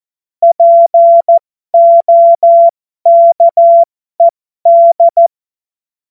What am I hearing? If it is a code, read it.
Morse code, heard as a keyed tone, about 14 wpm, spaced as for 12 wpm, 680 Hz. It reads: POKED